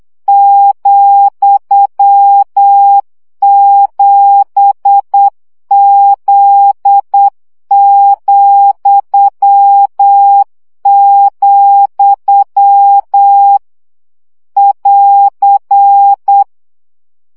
Morse code, heard as a keyed tone, about 8 wpm, 800 Hz. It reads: ,7Z,, <AR>
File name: CW gyakorlas z 7 vesszo 7 baud